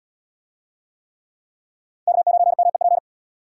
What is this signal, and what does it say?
Morse code, heard as a keyed tone, about 72 wpm, 700 Hz. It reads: K9MEP